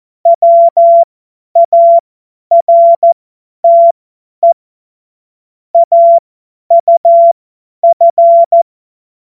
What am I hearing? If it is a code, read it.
Morse code, heard as a keyed tone, about 14 wpm, spaced as for 10 wpm, 680 Hz. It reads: WARTE AUF
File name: Morsekurs / Witze / Witz 02 Januar 2024 50 F70